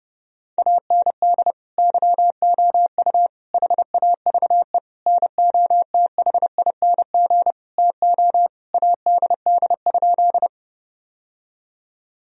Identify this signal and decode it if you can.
Morse code, heard as a keyed tone, about 30 wpm, 700 Hz. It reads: AND YOU HAVE NOTHING TO ADD?
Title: Sentences from Top 1000 Words - Review Part 1 - 30wpm